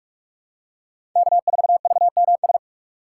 Morse code, heard as a keyed tone, about 46 wpm, 700 Hz. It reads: K4VMS
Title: USA Call Signs - 5 Characters - 2x Character Spacing - 45wpm